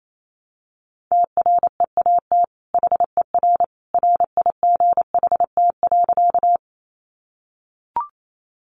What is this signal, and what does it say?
Morse code, heard as a keyed tone, about 28 wpm, 700 Hz. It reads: TREAT HER RIGHT.
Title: Sentences from Top 1300 Words 28wpm